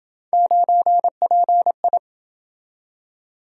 Morse code, heard as a keyed tone, about 28 wpm, 700 Hz. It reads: 9PI